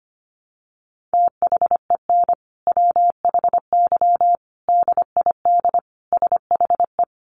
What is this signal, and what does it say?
Morse code, heard as a keyed tone, about 25 wpm, 700 Hz. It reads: THEN WHY DID SHE
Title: Sentences from Top 800 Words 25wpm